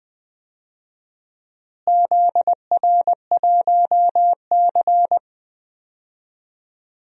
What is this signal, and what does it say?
Morse code, heard as a keyed tone, about 20 wpm, 700 Hz. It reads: ZR1C
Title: DXCC-100 Call Signs - 4 Characters - Repeated 3x - 20wpm